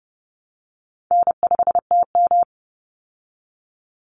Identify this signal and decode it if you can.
Morse code, heard as a keyed tone, about 30 wpm, 700 Hz. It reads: N5TM
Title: Easy Call Signs Repeated 3x - 30wpm